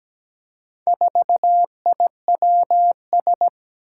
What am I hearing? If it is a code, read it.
Morse code, heard as a keyed tone, about 17 wpm, 700 Hz. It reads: 4IWS
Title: Four Challenging Letter-Number Combinations 17wpm